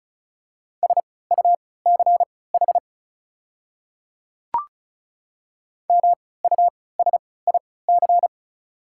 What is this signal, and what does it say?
Morse code, heard as a keyed tone, about 36 wpm, spaced as for 20 wpm, 690 Hz. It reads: SUCH MUSIC
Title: 3x Character Spacing - Top 300 Words - 35wpm